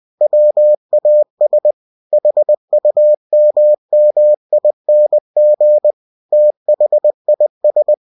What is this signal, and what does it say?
Morse code, heard as a keyed tone, about 20 wpm, 600 Hz. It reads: WAS HUMMING THIS
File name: cw-books / winnie-the-pooh / winnie-20wpm-page-0013